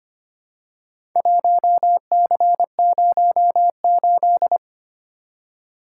Morse code, heard as a keyed tone, about 25 wpm, 700 Hz. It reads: 1C08
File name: practice / 25wpm / Four Letter-Number Combinations - 25wpm